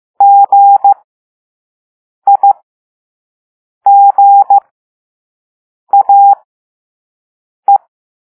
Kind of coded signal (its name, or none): Morse code